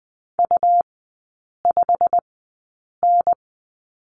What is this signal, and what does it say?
Morse code, heard as a keyed tone, about 20 wpm, spaced as for 8 wpm, 700 Hz. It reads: U5N
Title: 100 Lesson 17e - Letter u - Three Characters 20wpm